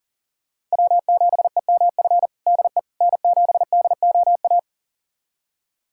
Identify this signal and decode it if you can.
Morse code, heard as a keyed tone, about 37 wpm, 700 Hz. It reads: W7EMF DE N7DOA
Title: USA Callsign DE Callsign - 5 Characters - 40wpm